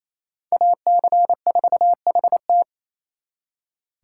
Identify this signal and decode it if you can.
Morse code, heard as a keyed tone, about 32 wpm, 700 Hz. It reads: AC4HT
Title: Speed Racing - USA Call Signs 28wpm